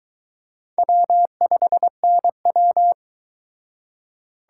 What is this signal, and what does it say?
Morse code, heard as a keyed tone, about 24 wpm, 700 Hz. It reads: W5NW